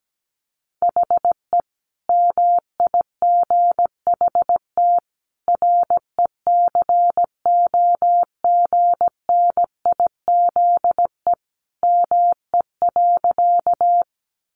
Morse code, heard as a keyed tone, about 17 wpm, 700 Hz. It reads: HE MIGHT RECOGNIZE ME.
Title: Sentences from Top 1400 Words 17wpm